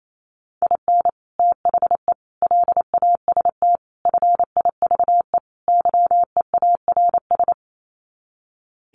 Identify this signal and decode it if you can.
Morse code, heard as a keyed tone, about 28 wpm, 700 Hz. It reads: IN THE LAST FIVE YEARS
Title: Sets of 5 Words Encore 28wpm